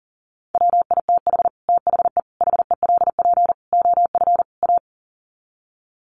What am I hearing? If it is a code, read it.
Morse code, heard as a keyed tone, about 40 wpm, 700 Hz. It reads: WITH THE HELP OF A
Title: Sets of 5 Words 40wpm